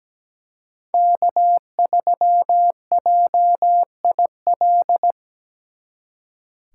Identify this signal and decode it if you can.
Morse code, heard as a keyed tone, about 20 wpm, 700 Hz. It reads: K3JIL
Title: Speed Racing - USA 1x3 Call Signs 17wpm